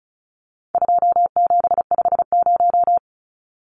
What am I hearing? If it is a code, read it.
Morse code, heard as a keyed tone, about 34 wpm, 700 Hz. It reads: 2750
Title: Numbers - 4-Digit 35wpm